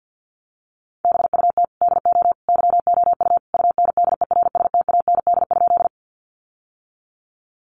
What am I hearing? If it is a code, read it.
Morse code, heard as a keyed tone, about 50 wpm, 700 Hz. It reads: BUT DO YOU UNDERSTAND?
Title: Sentences from Top 600 Words 50wpm